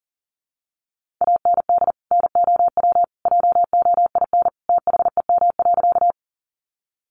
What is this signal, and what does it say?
Morse code, heard as a keyed tone, about 40 wpm, 700 Hz. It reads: AND NOW JOIN THEM.
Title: Sentences from Top 1400 Words 40wpm